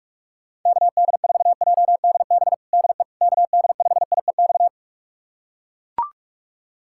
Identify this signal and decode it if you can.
Morse code, heard as a keyed tone, about 42 wpm, 700 Hz. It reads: KD4JDB DE KD5IEX
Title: USA Callsign DE Callsign 45wpm